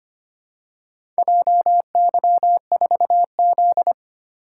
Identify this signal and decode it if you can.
Morse code, heard as a keyed tone, about 25 wpm, 700 Hz. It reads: JY4Z
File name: practice / 25wpm / Four Letter-Number Combinations - 25wpm